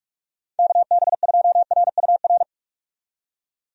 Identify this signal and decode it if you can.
Morse code, heard as a keyed tone, about 44 wpm, 700 Hz. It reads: KB2RUR
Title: USA Call Signs - 6 Characters - 45wpm